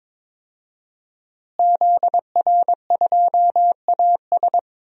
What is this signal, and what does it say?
Morse code, heard as a keyed tone, about 21 wpm, 700 Hz. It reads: ZR2AS
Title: DXCC-100 Call Signs - 5 Characters - Repeated 3x - 22wpm